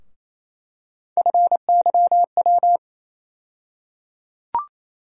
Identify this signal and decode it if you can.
Morse code, heard as a keyed tone, about 28 wpm, 700 Hz. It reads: FYW
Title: USA Call Sign Postfixes - 3 Letters 28wpm